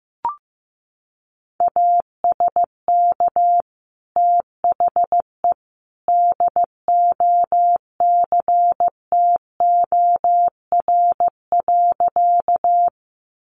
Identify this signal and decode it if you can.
Morse code, heard as a keyed tone, about 15 wpm, 700 Hz. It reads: ASK THE DOCTOR.
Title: Sentences from Top 1500 Words 15wpm